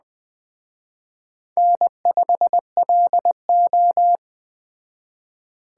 Morse code, heard as a keyed tone, about 21 wpm, 700 Hz. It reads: N5LO
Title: Speed Racing - USA 1x2 Call Signs 17wpm